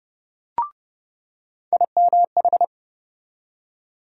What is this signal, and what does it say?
Morse code, heard as a keyed tone, about 30 wpm, 700 Hz. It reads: IMH
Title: USA Call Sign Postfixes - 3 Letters 30wpm